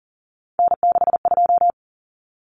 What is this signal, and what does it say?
Morse code, heard as a keyed tone, about 40 wpm, 700 Hz. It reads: N62
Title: Common Cut Numbers - 3-Digit 40wpm